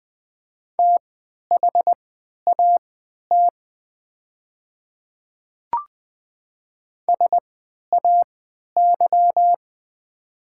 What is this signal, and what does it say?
Morse code, heard as a keyed tone, about 20 wpm, spaced as for 11 wpm, 690 Hz. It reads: THAT SAY
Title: 3x Character Spacing - Top 200 Words - 20wpm